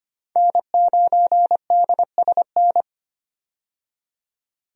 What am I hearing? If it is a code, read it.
Morse code, heard as a keyed tone, about 25 wpm, 700 Hz. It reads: N9DSN
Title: USA Call Signs - 5 Characters - 25wpm